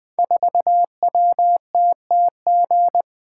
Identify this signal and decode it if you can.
Morse code, heard as a keyed tone, about 20 wpm, 700 Hz. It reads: 4WTTG
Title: Code Groups - 5 Letter:Numbers - 20wpm